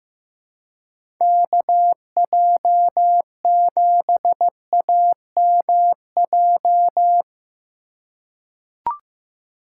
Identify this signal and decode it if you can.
Morse code, heard as a keyed tone, about 15 wpm, 700 Hz. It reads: KJ7AMJ
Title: USA Call Signs - 2x Character Spacing - 15wpm